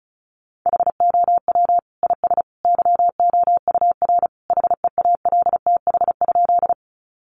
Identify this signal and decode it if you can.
Morse code, heard as a keyed tone, about 34 wpm, 700 Hz. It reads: HOW IS YOUR HEALTH?
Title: Sentences from Top 600 Words 35wpm